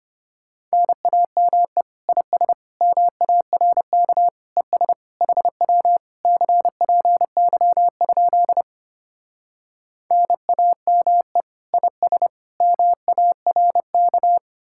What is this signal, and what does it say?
Morse code, heard as a keyed tone, about 31 wpm, 700 Hz. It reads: NAME IS MARK ES HW CPY? NAME IS MARK